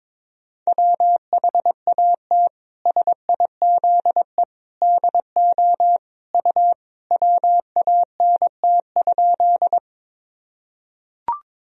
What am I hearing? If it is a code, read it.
Morse code, heard as a keyed tone, about 22 wpm, 700 Hz. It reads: WHAT SIZE DO U WANT?
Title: Sentences from Top 400 Words and Abbreviations 22wpm